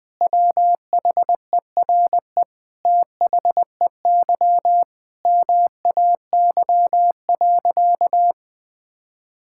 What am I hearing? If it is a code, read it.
Morse code, heard as a keyed tone, about 21 wpm, 700 Hz. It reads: WHERE THEY MAY.
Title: Sentences with 8-Word Phrases - 20wpm